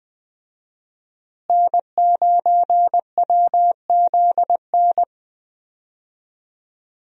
Morse code, heard as a keyed tone, about 19 wpm, 700 Hz. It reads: N9WZN
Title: USA Call Signs - 5 Characters - Repeated 3x - 20wpm